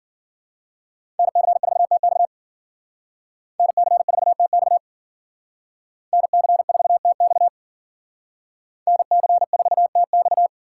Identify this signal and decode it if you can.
Morse code, heard as a keyed tone, about 48 wpm, 700 Hz. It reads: NC4TX NC4TX NC4TX NC4TX
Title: Speed Racing - USA 2x2 Call Signs 40wpm